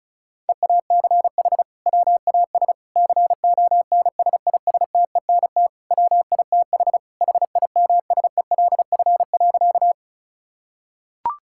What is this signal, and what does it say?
Morse code, heard as a keyed tone, about 36 wpm, 700 Hz. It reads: EACH WAS CONSISTENT WITH HIMSELF.
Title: Sentences from Top 1600 Words 35wpm